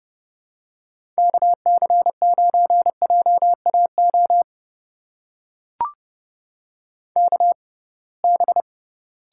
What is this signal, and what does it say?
Morse code, heard as a keyed tone, about 30 wpm, spaced as for 23 wpm, 700 Hz. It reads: KC9JAO KB
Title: USA Call Signs - 6 Characters - 6x Character Spacing - 30wpm